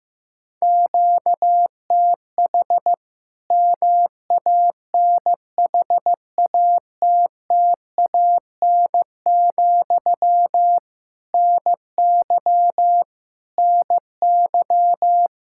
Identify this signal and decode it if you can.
Morse code, heard as a keyed tone, about 15 wpm, 700 Hz. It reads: QTH MANHATTAN, NY NY
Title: Practice QSO - Exchange 1 - Beginning - 15wpm